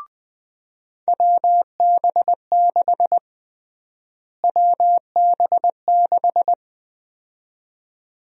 Morse code, heard as a keyed tone, about 21 wpm, 700 Hz. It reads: WB6 WB6
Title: USA Call Sign Prefixes - Repeated 2x - 20wpm